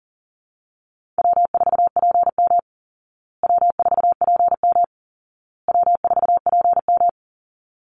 Morse code, heard as a keyed tone, about 40 wpm, 700 Hz. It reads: W4PM W4PM W4PM
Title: Easy Call Signs Repeated 3x - 40wpm